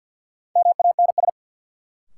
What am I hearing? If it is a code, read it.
Morse code, heard as a keyed tone, about 47 wpm, 700 Hz. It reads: MANS
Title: Sentences from Top 1000 Words - Review Part 6 - New Word Drill - 50wpm